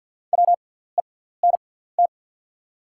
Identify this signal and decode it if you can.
Morse code, heard as a keyed tone, about 54 wpm, spaced as for 18 wpm, 690 Hz. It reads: WENT
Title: 6x Character Spacing - Top 200 Words - 50wpm